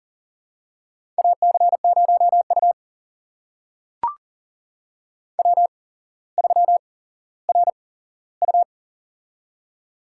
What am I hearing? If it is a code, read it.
Morse code, heard as a keyed tone, about 40 wpm, spaced as for 17 wpm, 700 Hz. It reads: AC0U W3RU
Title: USA Call Signs - 8x Character Spacing - 40wpm